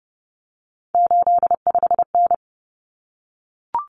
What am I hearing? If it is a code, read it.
Morse code, heard as a keyed tone, about 30 wpm, 700 Hz. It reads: 85N
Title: Common Cut Numbers - 3-Digit 30wpm